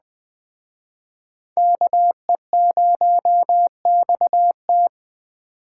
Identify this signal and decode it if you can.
Morse code, heard as a keyed tone, about 20 wpm, 700 Hz. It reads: KE0XT
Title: Speed Racing - USA 2x2 Call Signs 17wpm